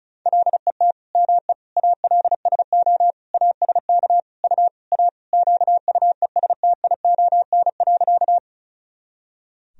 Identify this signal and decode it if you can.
Morse code, heard as a keyed tone, about 36 wpm, 700 Hz. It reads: LET ME ALSO ASK U A QUESTION.
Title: Sentences from Top 400 Words and Abbreviations 35wpm